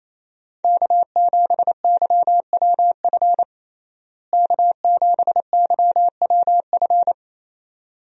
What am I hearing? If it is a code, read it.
Morse code, heard as a keyed tone, about 28 wpm, 700 Hz. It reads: K7YWF K7YWF
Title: USA Call Signs - 5 Characters - Repeated 2x - 28wpm